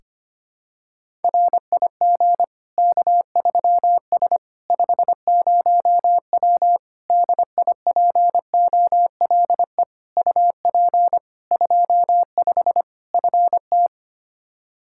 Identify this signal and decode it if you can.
Morse code, heard as a keyed tone, about 25 wpm, 700 Hz. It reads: RIG K3S 50W DIPOLE UP 25 FT
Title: Practice QSO - Exchange 2 - Info - 25wpm